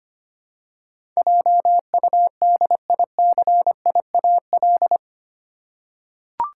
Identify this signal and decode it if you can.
Morse code, heard as a keyed tone, about 25 wpm, 700 Hz. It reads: JUDICIAL
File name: 8 Letter Words - First 3 Letters Given - 25wpm